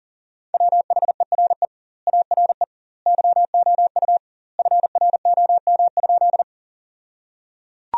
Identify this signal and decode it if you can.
Morse code, heard as a keyed tone, about 37 wpm, 700 Hz. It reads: WHERE ARE YOU FROM?
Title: 2x Word-Spacing - Sentences from Top 200 Words - 40wpm